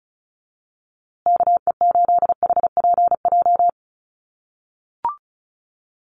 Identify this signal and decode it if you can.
Morse code, heard as a keyed tone, about 34 wpm, 700 Hz. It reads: KE8HPJ
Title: USA Call Signs Repeated 3x 35wpm